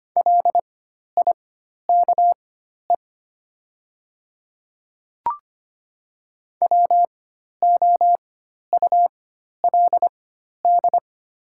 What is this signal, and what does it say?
Morse code, heard as a keyed tone, about 25 wpm, spaced as for 12 wpm, 700 Hz. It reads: LIKE WOULD
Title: 4x Character Spacing - Top 100 Words - 25wpm